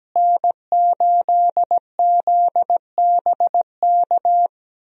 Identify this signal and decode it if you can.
Morse code, heard as a keyed tone, about 17 wpm, 700 Hz. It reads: N8ZBK
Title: USA Call Signs - 5 Characters - Repeated 3x - 17wpm